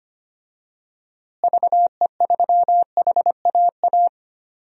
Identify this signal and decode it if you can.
Morse code, heard as a keyed tone, about 25 wpm, 700 Hz. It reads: VE3HAA
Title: DXCC-100 Call Signs - 6 Characters - Repeated 2x - 25wpm